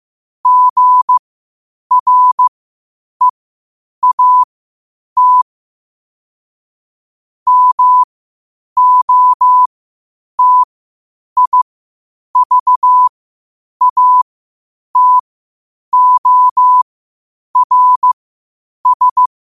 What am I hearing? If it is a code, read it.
Morse code, heard as a keyed tone, about 15 wpm, spaced as for 8 wpm, 1000 Hz. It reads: GREAT MOTIVATORS
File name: qotdfiles / 8wpm